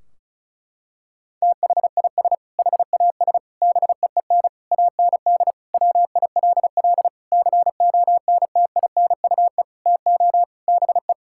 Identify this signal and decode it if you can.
Morse code, heard as a keyed tone, about 36 wpm, 700 Hz. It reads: THIS HAS BEEN AND WILL CONTINUE TO BE
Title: Sentences from Top 1600 Words 35wpm